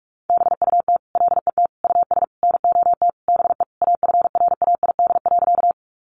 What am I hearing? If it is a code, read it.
Morse code, heard as a keyed tone, about 44 wpm, 700 Hz. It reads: BUT LET US NOT BE AFRAID.
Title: Sentences from Top 1500 Words 45wpm